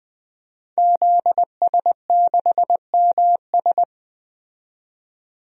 Morse code, heard as a keyed tone, about 20 wpm, 700 Hz. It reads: ZS6MS
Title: DXCC-100 Call Signs - 5 Characters - 20wpm